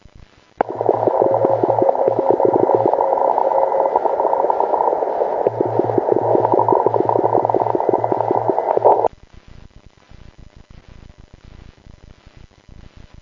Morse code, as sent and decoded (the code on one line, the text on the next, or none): none